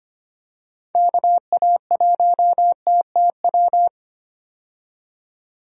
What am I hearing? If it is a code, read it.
Morse code, heard as a keyed tone, about 25 wpm, 700 Hz. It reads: KA1TTW
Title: USA Call Signs - 6 Characters - 25wpm